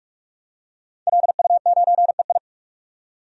Morse code, heard as a keyed tone, about 46 wpm, 700 Hz. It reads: LU9EI
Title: DXCC-100 Call Signs - 5 Characters - 45wpm